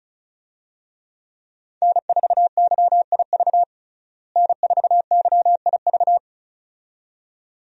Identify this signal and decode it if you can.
Morse code, heard as a keyed tone, about 36 wpm, 700 Hz. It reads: N4YIV N4YIV
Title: USA Call Signs - 5 Characters - Repeated 2x - 35wpm